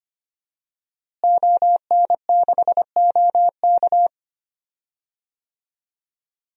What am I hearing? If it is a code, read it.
Morse code, heard as a keyed tone, about 25 wpm, 700 Hz. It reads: ON6OK
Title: DXCC-100 Call Signs - 5 Characters - Repeated 2x - 25wpm